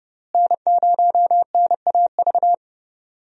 Morse code, heard as a keyed tone, about 29 wpm, 700 Hz. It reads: N0NAV
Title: USA Call Signs - 5 Characters - Repeated 3x - 30wpm